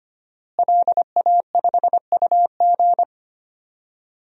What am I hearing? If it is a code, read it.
Morse code, heard as a keyed tone, about 25 wpm, 700 Hz. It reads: LA5UG
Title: DXCC-100 Call Signs - 5 Characters - Repeated 2x - 25wpm